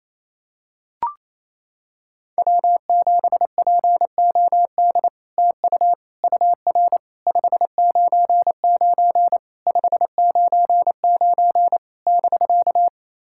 Morse code, heard as a keyed tone, about 27 wpm, 700 Hz. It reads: W7POD TU UR 599 599 <BK>
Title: SOTA QSO Practice 28wpm